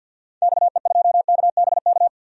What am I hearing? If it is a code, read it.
Morse code, heard as a keyed tone, about 46 wpm, 700 Hz. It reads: XE2KBK
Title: DXCC-100 Call Signs - 6 Characters - 50wpm